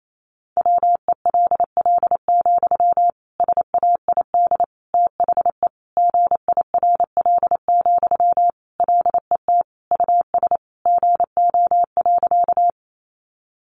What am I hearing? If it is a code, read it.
Morse code, heard as a keyed tone, about 28 wpm, 700 Hz. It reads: WELL, SAID THE GIRL, LET US GO.
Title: Sentences from Top 400 Words 28wpm